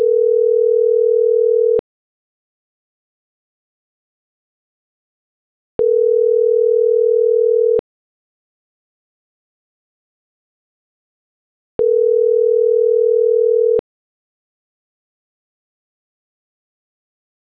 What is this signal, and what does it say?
Ringback tone (North America), 440 + 480 Hz.
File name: CENTRAL NMP / out isabelponciano